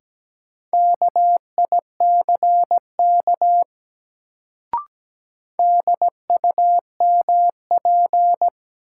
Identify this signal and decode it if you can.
Morse code, heard as a keyed tone, about 17 wpm, 700 Hz. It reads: KICK DUMP
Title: Sentences from Top 1300 Words - New Word Drill Alt - 17wpm